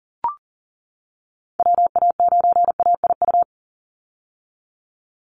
Morse code, heard as a keyed tone, about 40 wpm, 700 Hz. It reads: WA9AIU
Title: USA Call Signs 40wpm